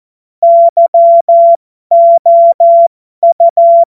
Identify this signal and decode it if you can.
Morse code, heard as a keyed tone, about 14 wpm, spaced as for 12 wpm, 680 Hz. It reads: YOU
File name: Morsekurs / Witze